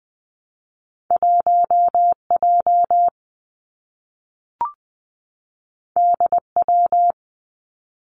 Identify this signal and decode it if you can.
Morse code, heard as a keyed tone, about 20 wpm, 700 Hz. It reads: 1 J D W
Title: Two Challenging Letter-Number Combinations 20wpm